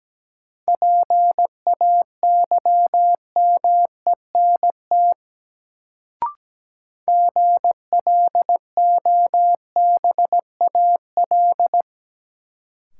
Morse code, heard as a keyed tone, about 17 wpm, 700 Hz. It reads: PAYMENT GLOBAL
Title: Sentences from Top 1200 Words - New Word Drill - 17wpm